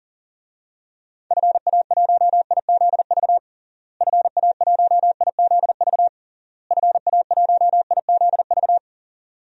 Morse code, heard as a keyed tone, about 40 wpm, 700 Hz. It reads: FA1IZV FA1IZV FA1IZV
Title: DXCC-100 Call Signs - 6 Characters - Repeated 3x - 40wpm